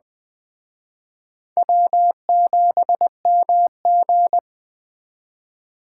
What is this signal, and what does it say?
Morse code, heard as a keyed tone, about 21 wpm, 700 Hz. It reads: W7MG
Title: Speed Racing - USA 1x2 Call Signs 17wpm